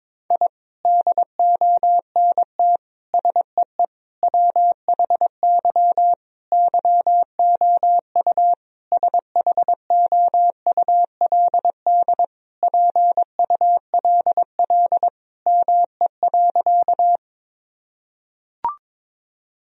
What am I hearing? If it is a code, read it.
Morse code, heard as a keyed tone, about 21 wpm, 700 Hz. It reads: I DONT SEE WHY YOU SHOULD PULL ME.
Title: Sentences from Top 1000 Words - Review Part 2 - 22wpm